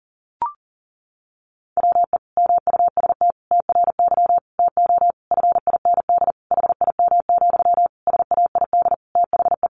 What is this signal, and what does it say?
Morse code, heard as a keyed tone, about 40 wpm, 700 Hz. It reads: WE MUST TRY TO FIND HIM, SAID THE
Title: Sentences from Top 400 Words 40wpm